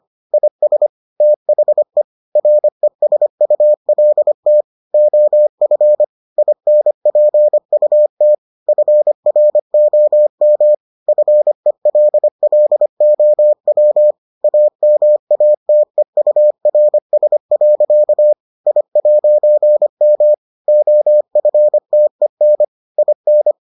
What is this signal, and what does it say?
Morse code, heard as a keyed tone, about 25 wpm, 600 Hz. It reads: IS THE RESULT OF INPUT FROM FELLOW AMATEURS. I'M OFTEN IN